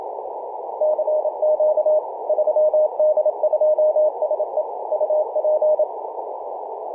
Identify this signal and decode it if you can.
Morse code, heard as a keyed tone, about 27 wpm, 610 Hz, with radio noise. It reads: CQ 3D2SE UP
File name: 3d2se-18-080mhz-cw